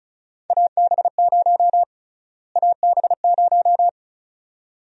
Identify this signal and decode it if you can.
Morse code, heard as a keyed tone, about 36 wpm, 700 Hz. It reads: AB0 AB0
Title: USA Call Sign Prefixes - Repeated 2x - 35wpm